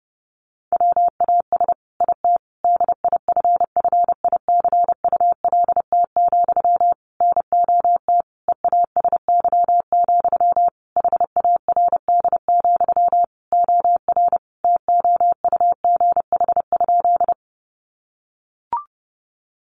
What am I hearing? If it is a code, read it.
Morse code, heard as a keyed tone, about 30 wpm, 700 Hz. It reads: WAS IT DIFFICULT, NOT EASY, HARD, OR TOUGH?
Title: Sentences from Top 1200 Words 30wpm